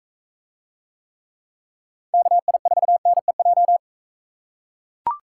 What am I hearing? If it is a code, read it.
Morse code, heard as a keyed tone, about 40 wpm, 700 Hz. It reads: KI4NEJ